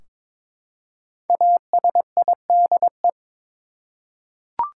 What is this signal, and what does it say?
Morse code, heard as a keyed tone, about 22 wpm, 700 Hz. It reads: ASIDE
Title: Five Letter Words - 22wpm